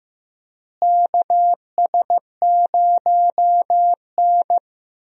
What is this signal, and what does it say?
Morse code, heard as a keyed tone, about 16 wpm, 700 Hz. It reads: KS0N
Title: USA Call Signs - 4 Characters - Repeated 2x - 15wpm